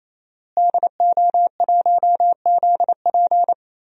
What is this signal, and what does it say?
Morse code, heard as a keyed tone, about 28 wpm, 700 Hz. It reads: DO1ZP
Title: DXCC-100 Call Signs - 5 Characters - Repeated 3x - 28wpm